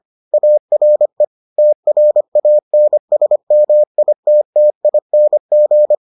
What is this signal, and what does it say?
Morse code, heard as a keyed tone, about 25 wpm, 600 Hz. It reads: ARE TRANSMITTING